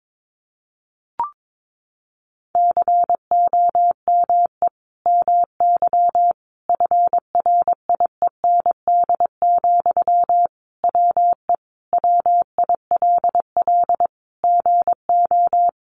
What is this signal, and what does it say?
Morse code, heard as a keyed tone, about 22 wpm, 700 Hz. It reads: COME MY FRIEND, WE WILL GO
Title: Sentences from Top 1300 Words 22wpm